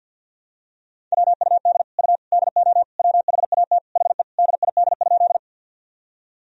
Morse code, heard as a keyed tone, about 52 wpm, 700 Hz. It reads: WUD U DO WHAT HE DID?
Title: Sentences from Top 100 Words and Abbreviations 50wpm